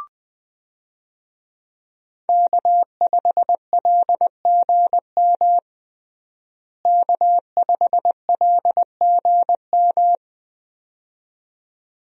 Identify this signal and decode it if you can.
Morse code, heard as a keyed tone, about 20 wpm, 700 Hz. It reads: K5LGM K5LGM
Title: USA Call Signs - 5 Characters - Repeated 2x - 20wpm